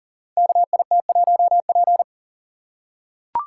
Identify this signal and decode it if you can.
Morse code, heard as a keyed tone, about 40 wpm, 700 Hz. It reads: KIT1P
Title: Code Groups - 5 Letter:Numbers - 40wpm